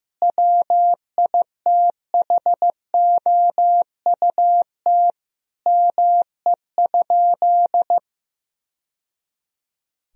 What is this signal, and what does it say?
Morse code, heard as a keyed tone, about 15 wpm, 700 Hz. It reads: WITHOUT ME?
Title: Sentences from Top 1000 Words - Review Part 3 - 15wpm